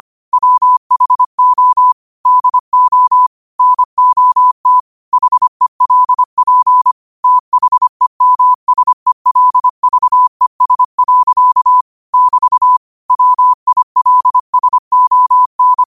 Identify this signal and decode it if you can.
Morse code, heard as a keyed tone, about 25 wpm, 1000 Hz. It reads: WHO DO NOT HELP THEMSELVES. <BT> WILSON